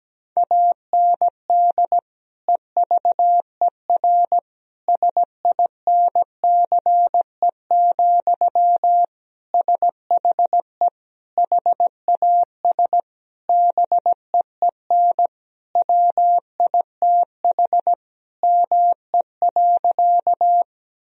Morse code, heard as a keyed tone, about 17 wpm, 700 Hz. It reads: AND EVER SINCE, SHE HAS BEEN WITH ME.
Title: Sentences from Top 1000 Words - Review Part 1 - 17wpm